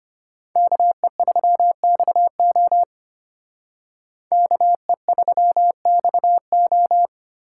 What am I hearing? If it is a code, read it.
Morse code, heard as a keyed tone, about 32 wpm, 700 Hz. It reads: KE3XO KE3XO